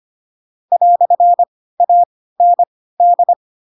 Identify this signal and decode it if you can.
Morse code, heard as a keyed tone, about 25 wpm, spaced as for 16 wpm, 700 Hz. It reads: "AND